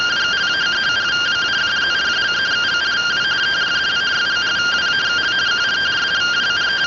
Two-tone teleprinter signal, about 170 Hz shift.